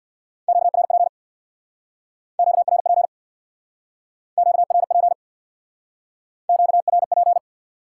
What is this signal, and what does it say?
Morse code, heard as a keyed tone, about 68 wpm, 700 Hz. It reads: QRP QRP QRP QRP